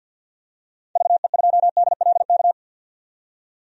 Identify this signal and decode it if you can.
Morse code, heard as a keyed tone, about 47 wpm, 700 Hz. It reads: VE2DLK
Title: DXCC-100 Call Signs - 6 Characters - 50wpm